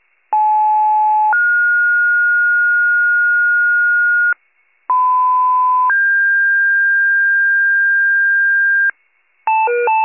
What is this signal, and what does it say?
Two-tone page, heard as a one-second tone, then a three-second tone, about 860 then 1430 Hz, about 1010 then 1600 Hz, with noise. It is heard from 0.3 to 8.9 s.
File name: pager / 75 FIRE / wcalert